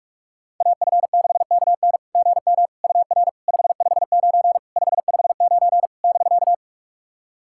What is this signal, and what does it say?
Morse code, heard as a keyed tone, about 41 wpm, 660 Hz. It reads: AF6KN GM UR 559 559 <BK>
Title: SOTA QSO Practice 45wpm